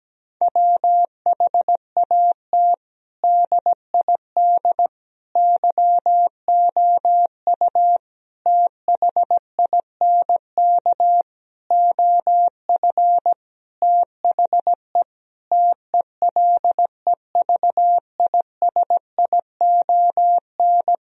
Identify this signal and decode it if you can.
Morse code, heard as a keyed tone, about 17 wpm, 700 Hz. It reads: WHAT DID YOU THINK OF THE TELEVISION
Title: Sentences from Top 1000 Words - Review Part 4 - 17wpm